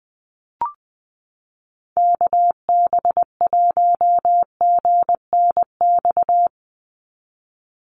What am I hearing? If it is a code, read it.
Morse code, heard as a keyed tone, about 20 wpm, 700 Hz. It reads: KB1GNX
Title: USA Call Signs 20wpm